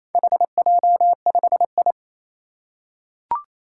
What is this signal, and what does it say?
Morse code, heard as a keyed tone, about 28 wpm, 700 Hz. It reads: HJ5I
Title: Four Challenging Letter-Number Combinations 28wpm